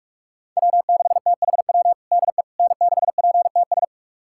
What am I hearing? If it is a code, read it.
Morse code, heard as a keyed tone, about 46 wpm, 700 Hz. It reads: W6THW DE N6PTS
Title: USA Callsign DE Callsign - 5 Characters - 45wpm